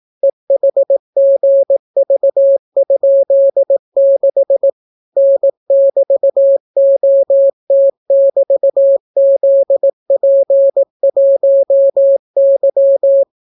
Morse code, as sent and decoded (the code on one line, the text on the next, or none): . .... --. ...- ..--.. -.... / -. -...- --- - -...- --.. .--. .---- -.--
EHGV?6 N<BT>OT<BT>ZP1Y